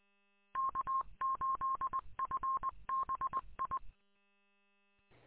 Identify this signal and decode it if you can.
Morse code, heard as a keyed tone, about 20 wpm, 1070 Hz. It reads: K8FBI
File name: k8fbi-43732-combined